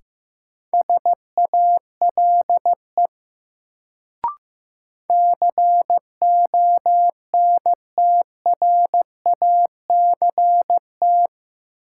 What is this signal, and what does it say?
Morse code, heard as a keyed tone, about 15 wpm, 700 Hz. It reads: SALE CONTRACT
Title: Sentences from Top 1400 Words - New Word Drill Alt - 15wpm